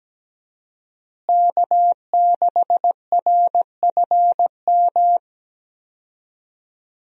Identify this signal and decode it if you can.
Morse code, heard as a keyed tone, about 17 wpm, 700 Hz. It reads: K6RFM
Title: USA Call Signs - 5 Characters - Repeated 2x - 17wpm